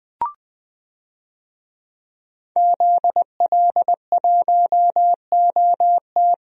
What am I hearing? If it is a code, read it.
Morse code, heard as a keyed tone, about 20 wpm, 700 Hz. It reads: ZL1OT
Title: DXCC-100 Call Signs - 5 Characters - Repeated 2x - 20wpm